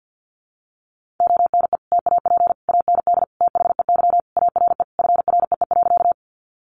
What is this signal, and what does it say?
Morse code, heard as a keyed tone, about 50 wpm, 700 Hz. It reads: ONE TAP AND THEY ARE FREE.
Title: Sentences from Top 1400 Words 50wpm